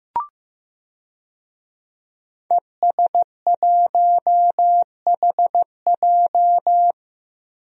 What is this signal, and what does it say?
Morse code, heard as a keyed tone, about 15 wpm, 700 Hz. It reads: ES1HJ
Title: DXCC-100 Call Signs - 5 Characters - Repeated 3x - 15wpm